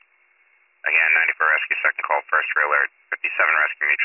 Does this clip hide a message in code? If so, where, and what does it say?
Yes: Emergency Alert System (SAME) tones from 1.1 to 3.8 s, heard as harsh buzzing bursts, with speech.